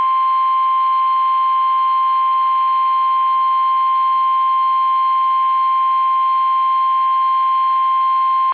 Emergency Alert System (SAME) tones (the 1050 Hz attention tone). It is heard from 1.0 to 7.8 s.